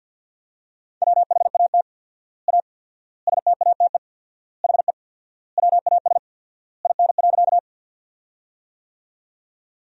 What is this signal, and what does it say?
Morse code, heard as a keyed tone, about 51 wpm, 700 Hz. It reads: WHAT A STATE HE WAS IN.
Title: 4x Word-Spacing - Sentences from Top 400 Words - 50wpm